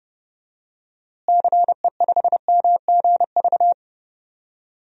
Morse code, heard as a keyed tone, about 32 wpm, 700 Hz. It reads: CE5MGV
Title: DXCC-100 Call Signs - 6 Characters - Repeated 3x - 30wpm